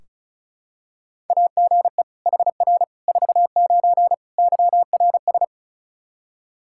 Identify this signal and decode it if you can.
Morse code, heard as a keyed tone, about 34 wpm, 700 Hz. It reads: AGE HR 49 YRS
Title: Practice QSO - Exchange 3 - More Info - 35wpm